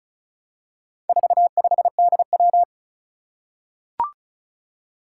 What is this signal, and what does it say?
Morse code, heard as a keyed tone, about 36 wpm, 700 Hz. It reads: 45DW
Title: Four Challenging Letter-Number Combinations 35wpm